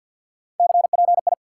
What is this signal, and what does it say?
Morse code, heard as a keyed tone, about 47 wpm, 700 Hz. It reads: CPI
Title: Top 180 QSO Elements 50wpm